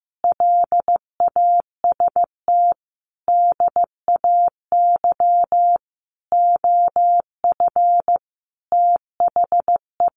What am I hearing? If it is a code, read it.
Morse code, heard as a keyed tone, about 15 wpm, 700 Hz. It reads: LAST DAY OF THE
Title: Sentences from Top 300 Words 15wpm